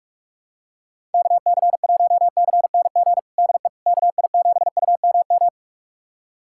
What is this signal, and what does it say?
Morse code, heard as a keyed tone, about 43 wpm, 700 Hz. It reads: KC1CNG DE KI7UMM
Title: USA Callsign DE Callsign 45wpm